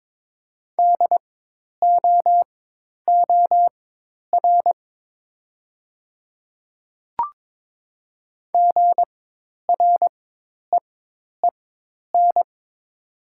Morse code, heard as a keyed tone, about 22 wpm, spaced as for 10 wpm, 700 Hz. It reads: DOOR GREEN
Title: 4x Character Spacing - Top 500 Words - 22wpm